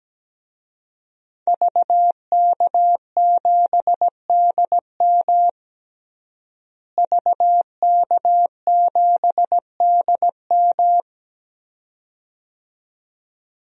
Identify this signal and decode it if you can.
Morse code, heard as a keyed tone, about 17 wpm, 700 Hz. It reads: VK7DM VK7DM
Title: DXCC-100 Call Signs - 5 Characters - Repeated 2x - 17wpm